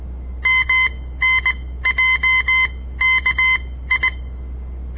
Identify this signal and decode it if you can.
Morse code, heard as a keyed tone, about 19 wpm, spaced as for 14 wpm, 2050 Hz. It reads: MNJKI
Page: On the air Morse code practice from the Southern Montana Amatuer Radio Association; all code, no read back